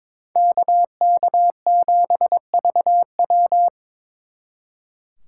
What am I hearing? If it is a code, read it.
Morse code, heard as a keyed tone, about 22 wpm, 700 Hz. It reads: KK7VW
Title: Speed Racing - USA 2x2 Call Signs 22wpm